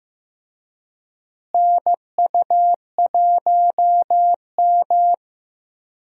Morse code, heard as a keyed tone, about 15 wpm, 700 Hz. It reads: NU1M
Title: USA Call Signs - 4 Characters - Repeated 3x - 15wpm